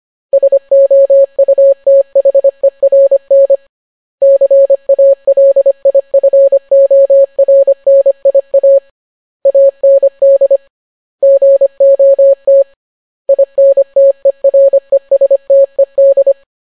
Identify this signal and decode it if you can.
Morse code, heard as a keyed tone, about 25 wpm, 550 Hz. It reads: SOUTHERN CALIFORNIA AND GOT INTERESTED